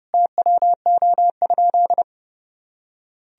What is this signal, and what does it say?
Morse code, heard as a keyed tone, about 30 wpm, 700 Hz. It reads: TWO?
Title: Sentences from Top 1000 Words - Review Part 1 - 30wpm